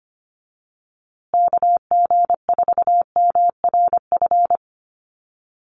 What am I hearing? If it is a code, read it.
Morse code, heard as a keyed tone, about 25 wpm, 700 Hz. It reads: KG4MRF
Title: USA Call Signs Repeated 2x 25wpm